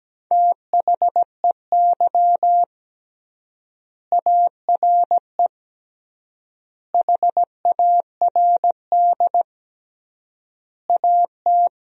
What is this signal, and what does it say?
Morse code, heard as a keyed tone, about 17 wpm, 700 Hz. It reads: THEY ARE HARD AT
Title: 3x Word-Spacing - Sentences from Top 300 Words - 17wpm